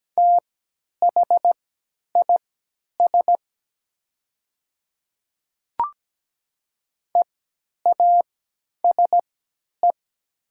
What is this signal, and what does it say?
Morse code, heard as a keyed tone, about 17 wpm, spaced as for 10 wpm, 700 Hz. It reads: THIS EASE
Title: 3x Character Spacing - Top 300 Words - 17wpm